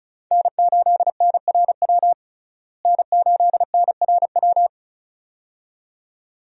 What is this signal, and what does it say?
Morse code, heard as a keyed tone, about 34 wpm, 700 Hz. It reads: N8NRW N8NRW
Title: USA Call Signs - 5 Characters - Repeated 2x - 35wpm